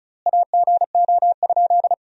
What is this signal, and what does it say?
Morse code, heard as a keyed tone, about 36 wpm, 700 Hz. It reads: AGO?
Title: 3x Word-Spacing - Sentences from Top 500 Words - 35wpm